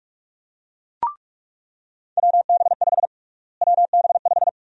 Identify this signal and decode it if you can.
Morse code, heard as a keyed tone, about 42 wpm, 700 Hz. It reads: WB5 WB5
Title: USA Call Sign Prefixes - Repeated 2x - 45wpm